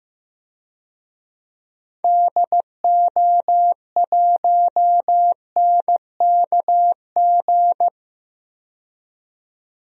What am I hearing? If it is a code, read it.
Morse code, heard as a keyed tone, about 15 wpm, 700 Hz. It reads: DO1NKG
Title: DXCC-100 Call Signs - 6 Characters - 15wpm